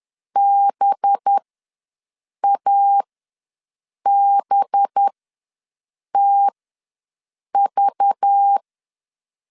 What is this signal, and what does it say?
Morse code, heard as a keyed tone, about 10 wpm, spaced as for 6 wpm, 800 Hz. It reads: BABTV
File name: LEZIONI / 5 LBVC / 5 LBV ws 10wpm 3645